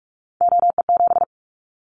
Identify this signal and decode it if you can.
Morse code, heard as a keyed tone, about 44 wpm, 700 Hz. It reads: OE7
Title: Callsign Prefixes for Top 100 DXCC - 45wpm